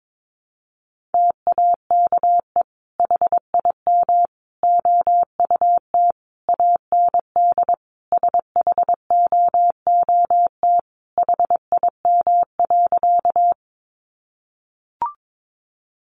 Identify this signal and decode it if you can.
Morse code, heard as a keyed tone, about 22 wpm, 700 Hz. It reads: TAKE HIM OUT AND SHOOT HIM.
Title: Sentences from Top 1200 Words 22wpm